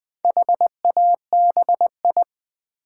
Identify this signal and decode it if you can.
Morse code, heard as a keyed tone, about 20 wpm, 700 Hz. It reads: HABI
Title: Build A Word - 1 Letter At A Time - 3 to 5-Letter Words - 20wpm